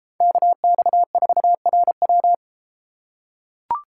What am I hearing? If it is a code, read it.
Morse code, heard as a keyed tone, about 32 wpm, 700 Hz. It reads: KX4RW